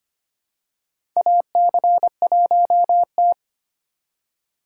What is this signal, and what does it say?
Morse code, heard as a keyed tone, about 25 wpm, 700 Hz. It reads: AC1T
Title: USA Call Signs - 2x Character Spacing - 25wpm